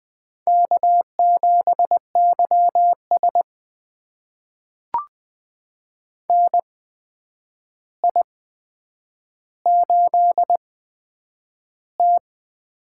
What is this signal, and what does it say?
Morse code, heard as a keyed tone, about 20 wpm, spaced as for 9 wpm, 700 Hz. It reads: K7YS NI8T
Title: USA Call Signs - 4 Characters - 8x Character Spacing - 20wpm